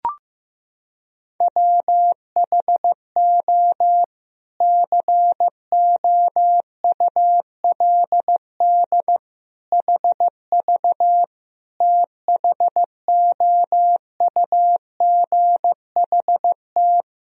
Morse code, heard as a keyed tone, about 16 wpm, 700 Hz. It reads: WHO COULD HV THOUGHT
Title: Sentences from Top 300 Words and Abbreviations 15wpm